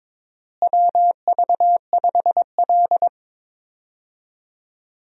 Morse code, heard as a keyed tone, about 22 wpm, 700 Hz. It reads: WV5L
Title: USA Call Signs - 4 Characters - Repeated 2x - 22wpm